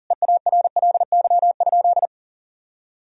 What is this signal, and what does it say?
Morse code, heard as a keyed tone, about 40 wpm, 700 Hz. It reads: EARLY?
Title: Sentences from Top 1000 Words - Review Part 2 - 40wpm